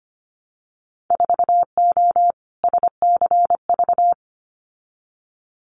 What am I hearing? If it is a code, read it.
Morse code, heard as a keyed tone, about 25 wpm, 700 Hz. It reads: 4O SCV
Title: Winter Field Day Exchanges 25wpm